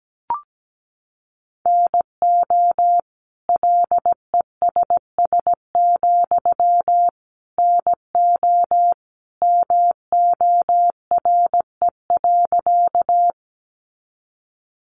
Morse code, heard as a keyed tone, about 17 wpm, 700 Hz. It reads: NO LESS, NO MORE.
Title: Sentences from Top 400 Words 17wpm